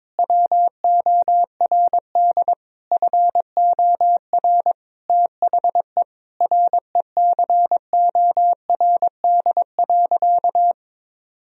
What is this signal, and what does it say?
Morse code, heard as a keyed tone, about 22 wpm, 700 Hz. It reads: WORD FOR THE RECORD.
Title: Sentences from Top 1000 Words - Review Part 3 - 22wpm